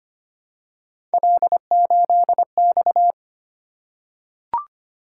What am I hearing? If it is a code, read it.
Morse code, heard as a keyed tone, about 25 wpm, 700 Hz. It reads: L8X
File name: Three Letter-Number Combinations - 25wpm